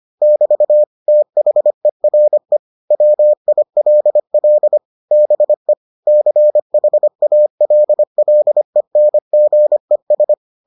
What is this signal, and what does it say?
Morse code, heard as a keyed tone, about 25 wpm, 600 Hz. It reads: <BT> THERE WILL BE CHALLENGES